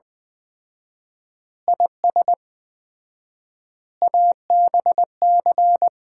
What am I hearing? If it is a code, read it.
Morse code, heard as a keyed tone, about 20 wpm, 700 Hz. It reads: IS ABC
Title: 4x Word-Spacing - Sets of 5 Words - 20wpm